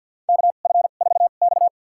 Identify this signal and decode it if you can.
Morse code, heard as a keyed tone, about 52 wpm, spaced as for 36 wpm, 700 Hz. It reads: KF4X
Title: USA Call Signs - 4 Characters - 2x Character Spacing - 50wpm